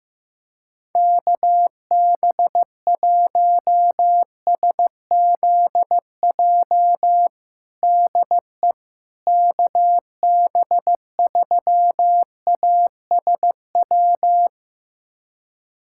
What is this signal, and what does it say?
Morse code, heard as a keyed tone, about 15 wpm, 700 Hz. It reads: KB1SZJ DE KB3ASW
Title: USA Callsign DE Callsign - 6 Characters - 15wpm